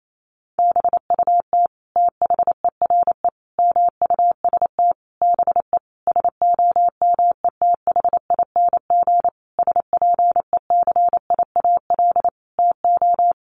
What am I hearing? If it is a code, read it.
Morse code, heard as a keyed tone, about 28 wpm, 700 Hz. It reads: BUT THERE MUST BE SOMETHING SPECIAL TO
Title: Sentences from Top 1500 Words 28wpm